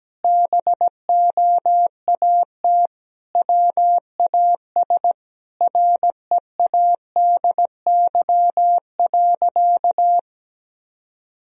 Morse code, heard as a keyed tone, about 17 wpm, 700 Hz. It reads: BOAT WAS READY.
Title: Sentences from Top 1000 Words - Review Part 3 - 17wpm